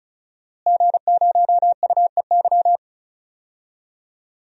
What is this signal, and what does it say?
Morse code, heard as a keyed tone, about 36 wpm, 700 Hz. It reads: G0UEY
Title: DXCC-100 Call Signs - 5 Characters - Repeated 2x - 35wpm